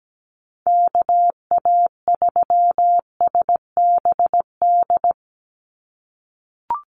Morse code, heard as a keyed tone, about 17 wpm, 700 Hz. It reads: KA3SBD
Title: USA Call Signs 17wpm